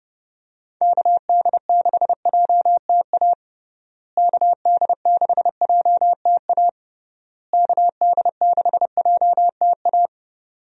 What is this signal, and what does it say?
Morse code, heard as a keyed tone, about 32 wpm, 700 Hz. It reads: KD6JTA KD6JTA KD6JTA
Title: USA Call Signs - 6 Characters - Repeated 3x - 30wpm